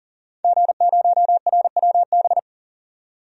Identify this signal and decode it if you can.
Morse code, heard as a keyed tone, about 40 wpm, 700 Hz. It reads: G0RWB
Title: DXCC-100 Call Signs - 5 Characters - 40wpm